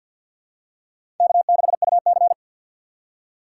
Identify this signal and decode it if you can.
Morse code, heard as a keyed tone, about 51 wpm, 700 Hz. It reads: K6UC
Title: USA Call Signs - 4 Characters - 2x Character Spacing - 50wpm